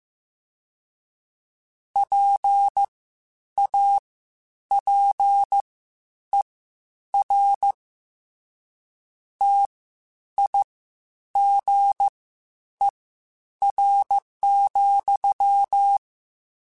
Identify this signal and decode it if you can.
Morse code, heard as a keyed tone, about 15 wpm, spaced as for 9 wpm, 800 Hz. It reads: PAPER TIGER,
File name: Misc / Misc-mp3s / WordPairs sp @13